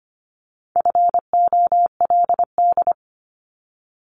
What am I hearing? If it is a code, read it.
Morse code, heard as a keyed tone, about 25 wpm, 700 Hz. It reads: FOLD
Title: Multi-letter Phonemes Repeated 3x 25wpm